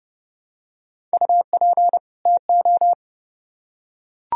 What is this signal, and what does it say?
Morse code, heard as a keyed tone, about 29 wpm, 700 Hz. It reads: UP TO